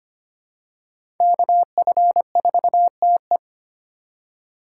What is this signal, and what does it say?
Morse code, heard as a keyed tone, about 25 wpm, 700 Hz. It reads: KF4TE
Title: USA Call Signs - 5 Characters - 4x Character Spacing - 25wpm